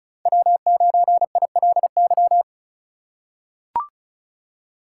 Morse code, heard as a keyed tone, about 34 wpm, 700 Hz. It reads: W9ILY
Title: K1USN SST QSO Practice - 35wpm